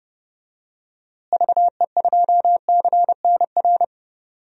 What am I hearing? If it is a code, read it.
Morse code, heard as a keyed tone, about 31 wpm, 700 Hz. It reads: VE2CNR
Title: DXCC-100 Call Signs - 6 Characters - Repeated 3x - 30wpm